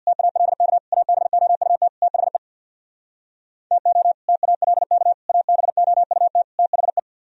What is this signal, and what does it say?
Morse code, heard as a keyed tone, about 64 wpm, 700 Hz. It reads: TALK ABOUT THE TO TALK ABOUT THE